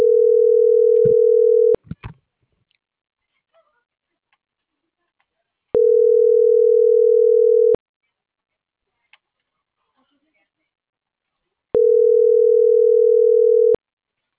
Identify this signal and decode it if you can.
Ringback tone (North America), 440 + 480 Hz.